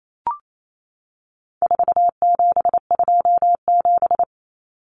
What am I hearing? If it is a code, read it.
Morse code, heard as a keyed tone, about 28 wpm, 700 Hz. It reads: 4727
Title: Numbers - 4-Digit 28wpm